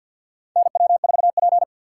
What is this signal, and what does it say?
Morse code, heard as a keyed tone, about 49 wpm, 700 Hz. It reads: NW4P
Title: USA Call Signs - 4 Characters - 3x Character Spacing - 50wpm